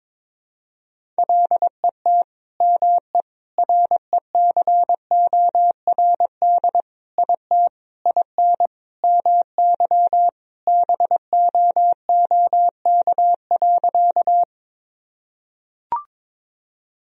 Morse code, heard as a keyed tone, about 22 wpm, 700 Hz. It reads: LET ME RECORD IT IN MY BOOK.
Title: Sentences from Top 1000 Words - Review Part 3 - 22wpm